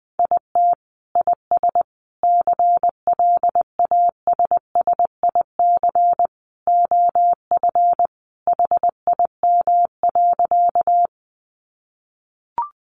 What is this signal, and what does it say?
Morse code, heard as a keyed tone, about 20 wpm, 700 Hz. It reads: IT IS CLASSIC OF HIM.
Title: Sentences from Top 1000 Words 20wpm